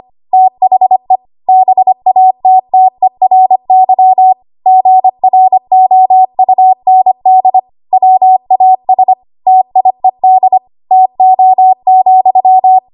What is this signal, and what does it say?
Morse code, heard as a keyed tone, about 25 wpm, 750 Hz. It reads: THE BATTERY GROUND WAS TIED TO,
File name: Archive / 25 WPM / 25WPM